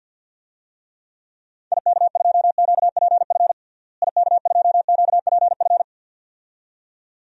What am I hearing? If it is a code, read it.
Morse code, heard as a keyed tone, about 48 wpm, 700 Hz. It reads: IK2QPF IK2QPF
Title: DXCC-100 Call Signs - 6 Characters - Repeated 2x - 50wpm